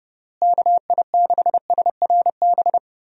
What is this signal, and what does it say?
Morse code, heard as a keyed tone, about 29 wpm, 700 Hz. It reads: KI6SRB
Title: USA Call Signs - 6 Characters - Repeated 3x - 30wpm